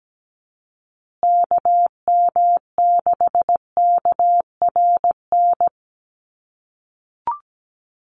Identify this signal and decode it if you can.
Morse code, heard as a keyed tone, about 17 wpm, 700 Hz. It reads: KM6KRN
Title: USA Call Signs Repeated 2x 17wpm